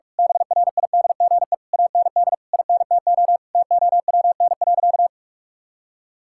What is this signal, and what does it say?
Morse code, heard as a keyed tone, about 47 wpm, 700 Hz. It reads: BRIDGE AND INTO TOWN.
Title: Sentences from Top 1600 Words 45wpm